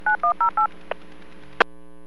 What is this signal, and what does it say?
DTMF, heard as telephone touch tones, tones of about 120 ms. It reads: 9108
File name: WPS / 00006 15-47 WPS 451.25 WPS